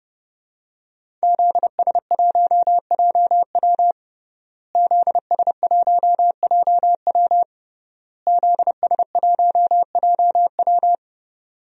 Morse code, heard as a keyed tone, about 31 wpm, 700 Hz. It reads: ZS1JW ZS1JW ZS1JW
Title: DXCC-100 Call Signs - 5 Characters - Repeated 3x - 30wpm